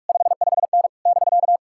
Morse code, heard as a keyed tone, about 55 wpm, 700 Hz. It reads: 55N <BK>
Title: Speed-Racing Common QSO Phrases - 45wpm